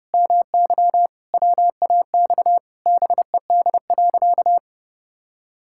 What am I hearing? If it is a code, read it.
Morse code, heard as a keyed tone, about 30 wpm, 700 Hz. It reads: MY WAX BED.
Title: Pangram Challenge v2 - 30wpm